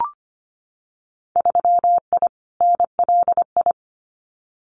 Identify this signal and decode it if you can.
Morse code, heard as a keyed tone, about 25 wpm, 700 Hz. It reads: 3I NLI
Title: Winter Field Day Exchanges 25wpm